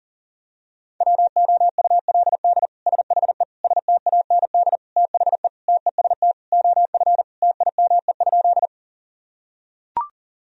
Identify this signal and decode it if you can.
Morse code, heard as a keyed tone, about 40 wpm, 700 Hz. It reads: WOULD SHE STAND THE TEST OF TIME?
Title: Sentences from Top 1000 Words - Review Part 3 - 40wpm